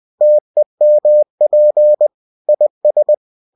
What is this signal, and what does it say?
Morse code, heard as a keyed tone, about 20 wpm, 600 Hz. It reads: TEMP IS